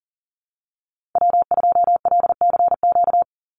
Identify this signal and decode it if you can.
Morse code, heard as a keyed tone, about 40 wpm, 700 Hz. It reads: W2LCQ
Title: Call Signs Repeated 3x - 40wpm